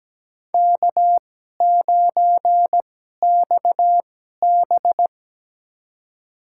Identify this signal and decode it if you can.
Morse code, heard as a keyed tone, about 17 wpm, spaced as for 12 wpm, 700 Hz. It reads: K9XB
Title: USA Call Signs - 2x Character Spacing - 17wpm